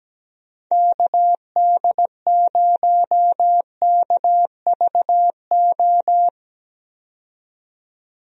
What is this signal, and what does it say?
Morse code, heard as a keyed tone, about 17 wpm, 700 Hz. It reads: KD0KVO
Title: USA Call Signs - 6 Characters - 17wpm